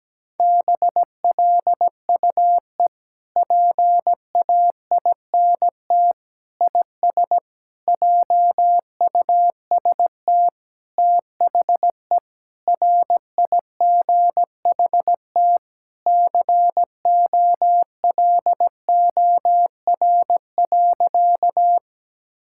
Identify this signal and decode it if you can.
Morse code, heard as a keyed tone, about 17 wpm, 700 Hz. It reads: BLUE PAINT IS JUST THE RIGHT COLOR.
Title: Sentences from Top 1000 Words - Review Part 6 - 17wpm